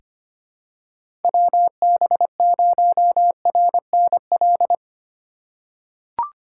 Morse code, heard as a keyed tone, about 25 wpm, 700 Hz. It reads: WB0RNL
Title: USA Call Signs - 6 Characters - 3x Character Spacing - 25wpm